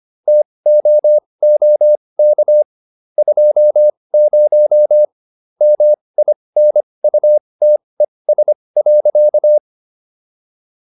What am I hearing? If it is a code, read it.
Morse code, heard as a keyed tone, about 25 wpm, spaced as for 20 wpm, 600 Hz. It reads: TOOK 20 MINUTES.